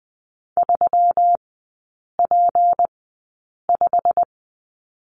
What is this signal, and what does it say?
Morse code, heard as a keyed tone, about 20 wpm, spaced as for 8 wpm, 700 Hz. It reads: 3P5
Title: 178 Lesson 29e - Slash - Three Characters 20wpm